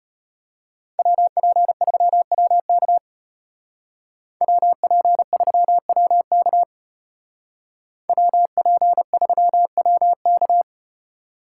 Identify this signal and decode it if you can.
Morse code, heard as a keyed tone, about 35 wpm, 700 Hz. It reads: WP3WK WP3WK WP3WK